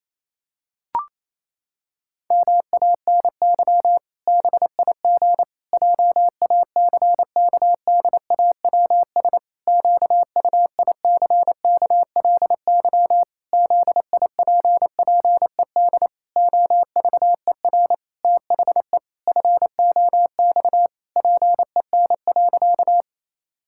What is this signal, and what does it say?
Morse code, heard as a keyed tone, about 28 wpm, 700 Hz. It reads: MANY BIG JACKDAWS QUICKLY ZIPPED OVER THE FOX PEN.
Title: Pangram Challenge v2 - 28wpm